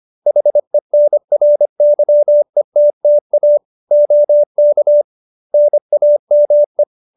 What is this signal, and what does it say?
Morse code, heard as a keyed tone, about 25 wpm, 590 Hz. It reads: HENRYETTA OK NAME